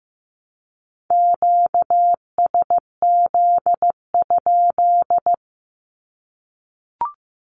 Morse code, heard as a keyed tone, about 15 wpm, 700 Hz. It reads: QSZ?
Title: Q-Codes Repeated 3x 15wpm